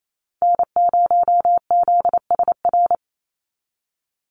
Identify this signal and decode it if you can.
Morse code, heard as a keyed tone, about 28 wpm, 700 Hz. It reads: N0ZSR
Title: USA Call Signs 28wpm